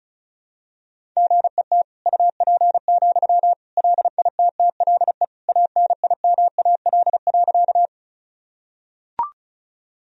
Morse code, heard as a keyed tone, about 36 wpm, 700 Hz. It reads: GET UP, LITTLE ANIMAL.
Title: Sentences from Top 1000 Words - Review Part 1 - 35wpm